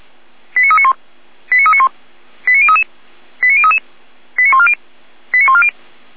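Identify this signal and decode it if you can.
Five-tone selective call (ZVEI), heard as quick runs of five tones, tones of about 70 ms. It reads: 89391 89391 89033 89033 89140 89140
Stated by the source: Five-tone ZVEI selective call received by SDR.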